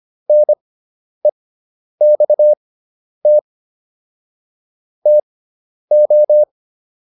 Morse code, heard as a keyed tone, about 25 wpm, spaced as for 10 wpm, 600 Hz. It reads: NEXT TO